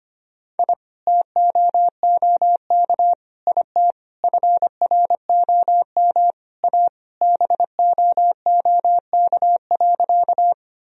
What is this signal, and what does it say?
Morse code, heard as a keyed tone, about 25 wpm, 700 Hz. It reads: I TOOK IT FROM A BOOK.
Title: Sentences from Top 1000 Words - Review Part 1 - 25wpm